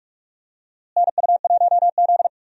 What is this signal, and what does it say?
Morse code, heard as a keyed tone, about 47 wpm, 700 Hz. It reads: NU1Z
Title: USA Call Signs - 4 Characters - Repeated 2x - 45wpm